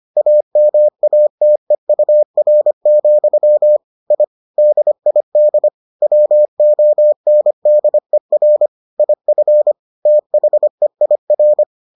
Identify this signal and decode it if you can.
Morse code, heard as a keyed tone, about 25 wpm, 600 Hz. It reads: AMATEUR, I DID WONDER IF THEIR